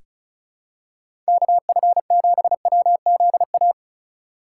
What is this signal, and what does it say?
Morse code, heard as a keyed tone, about 36 wpm, 700 Hz. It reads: KF7WZA
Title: USA Call Signs - 4x Character Spacing - 35wpm